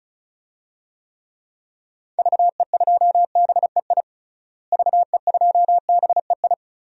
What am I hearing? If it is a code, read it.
Morse code, heard as a keyed tone, about 34 wpm, 700 Hz. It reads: VE2BEI VE2BEI
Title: DXCC-100 Call Signs - 6 Characters - Repeated 3x - 35wpm